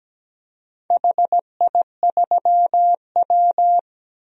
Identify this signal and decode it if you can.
Morse code, heard as a keyed tone, about 17 wpm, 700 Hz. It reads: HI3W
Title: DXCC-100 Call Signs - 4 Characters - 17wpm